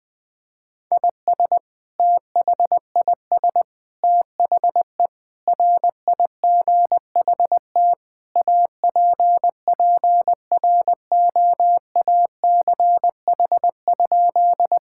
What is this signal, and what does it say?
Morse code, heard as a keyed tone, about 20 wpm, 700 Hz. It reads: IS THIS THE RIGHT APPROACH?
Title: Sentences from Top 1000 Words - Review Part 6 - 20wpm